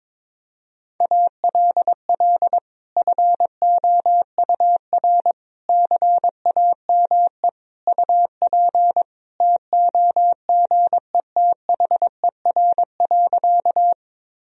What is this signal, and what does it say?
Morse code, heard as a keyed tone, about 22 wpm, 700 Hz. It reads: ALL FOUR CAME UP TOGETHER.
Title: Sentences from Top 1000 Words - Review Part 1 - 22wpm